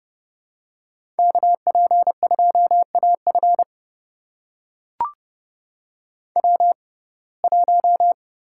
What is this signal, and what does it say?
Morse code, heard as a keyed tone, about 32 wpm, spaced as for 22 wpm, 700 Hz. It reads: KP2AF W1
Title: USA Call Signs - 5 Characters - 6x Character Spacing - 30wpm